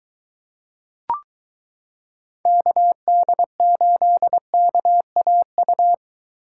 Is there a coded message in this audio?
Yes: Morse code from 2.5 to 5.9 s, heard as a keyed tone, about 23 wpm, 700 Hz. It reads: KD8KAU